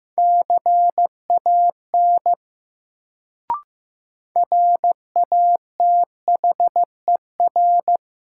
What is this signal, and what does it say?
Morse code, heard as a keyed tone, about 15 wpm, 700 Hz. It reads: CAN RATHER
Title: Sets of 2 Words - New Word Drill - 15wpm